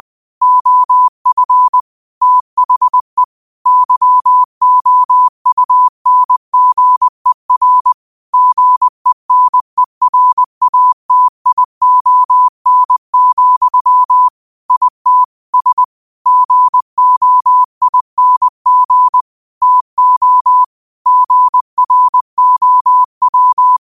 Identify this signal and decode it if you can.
Morse code, heard as a keyed tone, about 20 wpm, 1000 Hz. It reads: OF THE YOUNGER GENERATION, IT S GOING TO GROW
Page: Daily quotes in Morse Code at 20 Words per Minute. Use this to improve your code copying skill.